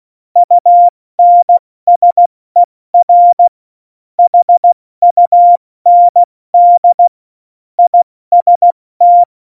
Morse code, heard as a keyed tone, about 16 wpm, spaced as for 14 wpm, 700 Hz. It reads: UNSER HUND IST